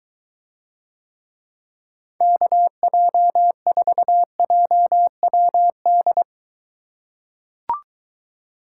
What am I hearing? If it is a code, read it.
Morse code, heard as a keyed tone, about 21 wpm, 700 Hz. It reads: KJ4JWD